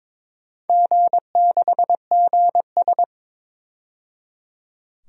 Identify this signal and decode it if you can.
Morse code, heard as a keyed tone, about 22 wpm, 700 Hz. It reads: G6GS
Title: DXCC-100 Call Signs - 4 Characters - 22wpm